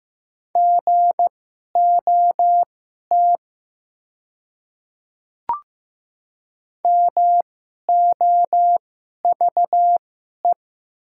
Morse code, heard as a keyed tone, about 15 wpm, spaced as for 11 wpm, 700 Hz. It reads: GOT MOVE
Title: 2x Character Spacing - Top 500 Words - 15wpm